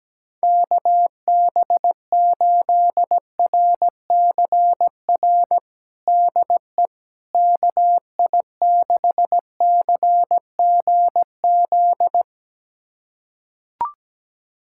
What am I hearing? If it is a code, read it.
Morse code, heard as a keyed tone, about 17 wpm, 700 Hz. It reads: KB8RCR DE KI6CGZ
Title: USA Callsign DE Callsign - 6 Characters - 17wpm